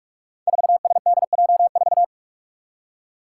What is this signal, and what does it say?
Morse code, heard as a keyed tone, about 42 wpm, 700 Hz. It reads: 4SDJ4
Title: Five Challenging Letter-Number Combinations 45wpm